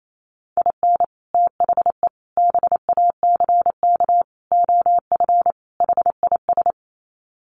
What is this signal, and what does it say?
Morse code, heard as a keyed tone, about 28 wpm, 700 Hz. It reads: IN THE BACK OF HIS
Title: Sets of 5 Words Encore2 28wpm